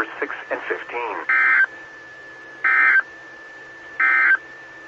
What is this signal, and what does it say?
Emergency Alert System (SAME) tones, heard as harsh buzzing bursts, with speech. It is heard from 1.2 to 4.5 s.
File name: RR Skywarn / W5RHS SKYWARN Repeater System